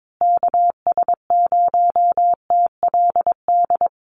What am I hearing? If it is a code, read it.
Morse code, heard as a keyed tone, about 22 wpm, 700 Hz. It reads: KS0TLD
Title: USA Call Signs Repeated 3x 22wpm